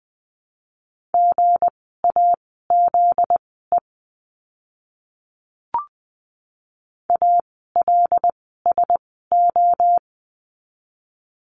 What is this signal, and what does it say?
Morse code, heard as a keyed tone, about 20 wpm, spaced as for 14 wpm, 700 Hz. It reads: GAZE ALSO
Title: 244 Lesson 39f - Letter z - Words 20wpm